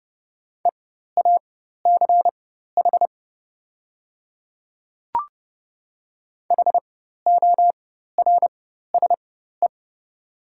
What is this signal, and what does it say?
Morse code, heard as a keyed tone, about 31 wpm, spaced as for 14 wpm, 700 Hz. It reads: EACH HORSE
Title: 4x Character Spacing - Top 300 Words - 30wpm